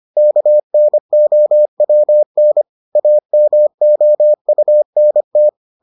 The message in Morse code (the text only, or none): KNOWN AMOUNT